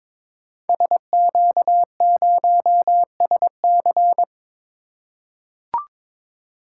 Morse code, heard as a keyed tone, about 22 wpm, 690 Hz. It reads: SQ0SC
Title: Code Groups - 5 Letter:Numbers - 22wpm